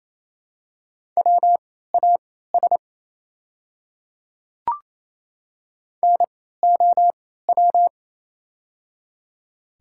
Morse code, heard as a keyed tone, about 28 wpm, spaced as for 16 wpm, 700 Hz. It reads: WAS NOW
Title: 3x Character Spacing - Top 100 Words - 28wpm